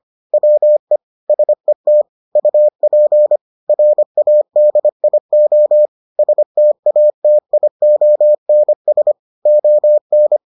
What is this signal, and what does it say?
Morse code, heard as a keyed tone, about 25 wpm, 600 Hz. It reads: WE SET UP RADIO STATIONS ON